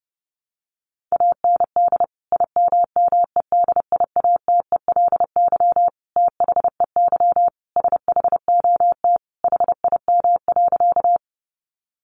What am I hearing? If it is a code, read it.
Morse code, heard as a keyed tone, about 30 wpm, 700 Hz. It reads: AND IMMEDIATELY THEY SHOT HIM.
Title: Sentences from Top 1100 Words 30wpm